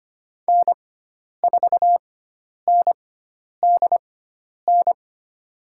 Morse code, heard as a keyed tone, about 25 wpm, spaced as for 10 wpm, 700 Hz. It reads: N4NDN
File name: Lesson 16g - Letter c - Call Signs 25wpm